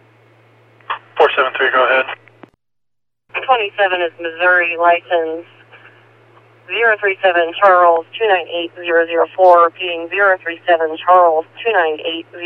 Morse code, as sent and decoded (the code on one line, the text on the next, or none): none